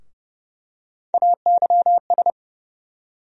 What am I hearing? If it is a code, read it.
Morse code, heard as a keyed tone, about 30 wpm, 700 Hz. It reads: AYS
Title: USA Call Sign Postfixes - 3 Letters 30wpm